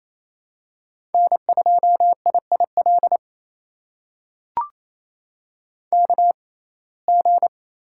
Morse code, heard as a keyed tone, about 28 wpm, spaced as for 20 wpm, 700 Hz. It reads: N2IIL KG
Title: USA Call Signs - 5 Characters - 6x Character Spacing - 28wpm